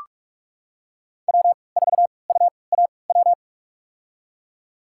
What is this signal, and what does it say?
Morse code, heard as a keyed tone, about 46 wpm, spaced as for 25 wpm, 700 Hz. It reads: W4UAW
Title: USA Call Signs - 3x Character Spacing - 45wpm